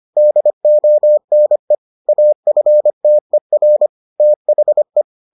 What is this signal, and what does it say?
Morse code, heard as a keyed tone, about 25 wpm, 600 Hz. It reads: DONE AFTER THE